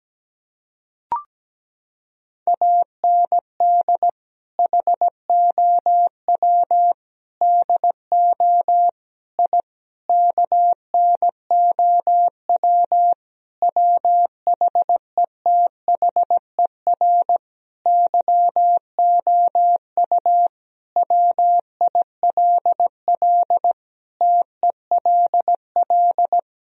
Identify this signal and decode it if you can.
Morse code, heard as a keyed tone, about 17 wpm, 700 Hz. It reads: AND HOW DO I KNOW WHETHER YOU WILL TELL
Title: Sentences from Top 1000 Words - Review Part 3 - 17wpm